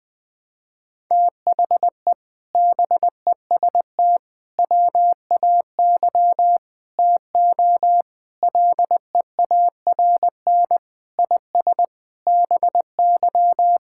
Morse code, heard as a keyed tone, about 21 wpm, 700 Hz. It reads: THE BEST WAY TO LEARN IS BY
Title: Sentences with 4-Word Phrases - 20wpm